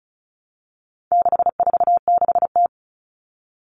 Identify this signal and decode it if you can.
Morse code, heard as a keyed tone, about 34 wpm, 700 Hz. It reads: 646T
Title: All Cut Numbers - 4-Digit 35wpm